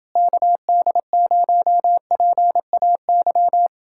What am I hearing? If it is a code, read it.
Morse code, heard as a keyed tone, about 29 wpm, 700 Hz. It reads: KD0PAY